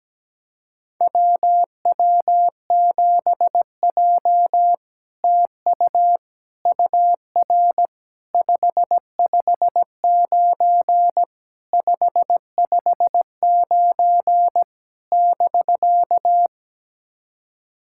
Morse code, heard as a keyed tone, about 17 wpm, 700 Hz. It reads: WW7J TU UR 559 559 <BK>
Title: SOTA QSO Practice 17wpm